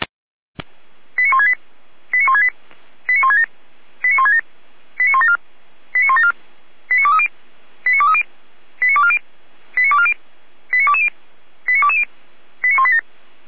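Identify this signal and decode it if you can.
Five-tone selective call (ZVEI), heard as quick runs of five tones, tones of about 70 ms. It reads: 89168 89168 89167 89167 89174 89174 89230 89230 89240 89240 89220 89220 89187
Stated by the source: ZVEI five-tone selective call received on SDR, from Feuerwehr Orlamünde.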